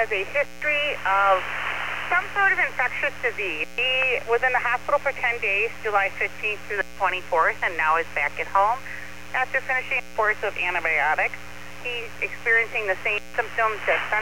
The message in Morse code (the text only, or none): none